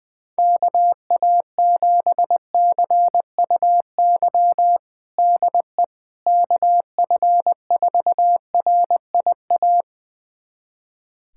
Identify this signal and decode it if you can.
Morse code, heard as a keyed tone, about 20 wpm, 700 Hz. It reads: KA7CUY DE KF4RIA
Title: USA Callsign DE Callsign 20wpm